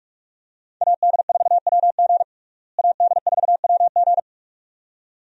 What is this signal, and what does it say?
Morse code, heard as a keyed tone, about 46 wpm, 700 Hz. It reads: AD4WG AD4WG
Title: USA Call Signs - 5 Characters - Repeated 2x - 45wpm